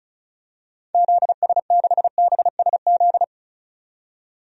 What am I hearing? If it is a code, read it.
Morse code, heard as a keyed tone, about 34 wpm, 700 Hz. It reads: ZS6BSZ
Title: DXCC-100 Call Signs - 6 Characters - 35wpm